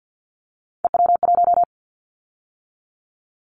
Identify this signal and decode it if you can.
Morse code, heard as a keyed tone, about 50 wpm, 700 Hz. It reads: EW1
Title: Callsign Prefixes for Top 50 DXCC - 50wpm